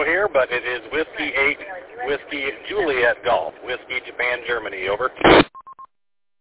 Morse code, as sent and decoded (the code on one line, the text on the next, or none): none